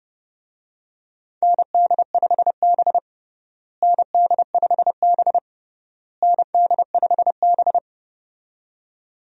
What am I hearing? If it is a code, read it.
Morse code, heard as a keyed tone, about 30 wpm, 700 Hz. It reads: ND5B ND5B ND5B
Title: USA Call Signs - 4 Characters - Repeated 3x - 30wpm